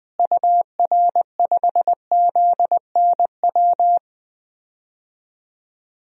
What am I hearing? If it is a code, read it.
Morse code, heard as a keyed tone, about 20 wpm, 700 Hz. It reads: UR5ZNW
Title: DXCC-100 Call Signs - 6 Characters - Repeated 3x - 20wpm